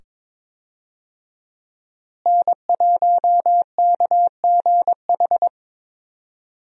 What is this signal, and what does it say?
Morse code, heard as a keyed tone, about 22 wpm, 700 Hz. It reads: N1KGH
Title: USA Call Signs - 5 Characters - Repeated 2x - 22wpm